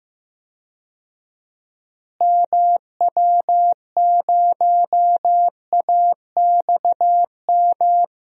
Morse code, heard as a keyed tone, about 15 wpm, 700 Hz. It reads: MW0AXM
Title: DXCC-100 Call Signs - 6 Characters - Repeated 2x - 15wpm